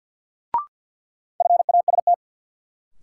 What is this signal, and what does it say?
Morse code, heard as a keyed tone, about 48 wpm, 700 Hz. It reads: FAST
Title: Sentences from Top 400 Words - New Word Drill - 50wpm